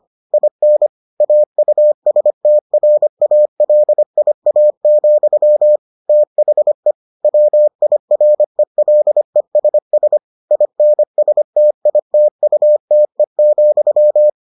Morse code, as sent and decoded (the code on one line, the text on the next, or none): .. -. / .- ..- ... - .-. .- .-.. .. .- --..-- / - .... / .-- .. .-. .-.. ... ... / .. -. ... - .. - ..- - . --..--
IN AUSTRALIA, THE WIRELESS INSTITUTE,